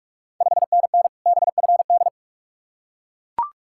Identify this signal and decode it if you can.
Morse code, heard as a keyed tone, about 44 wpm, 690 Hz. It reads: 5NN BFD
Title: Ontario QSO Party Exchange 45wpm